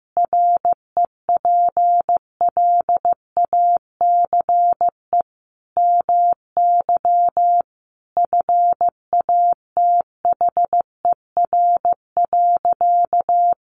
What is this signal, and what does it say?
Morse code, heard as a keyed tone, about 15 wpm, 700 Hz. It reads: REPLACE MY FATHER.
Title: Sentences from Top 1400 Words 15wpm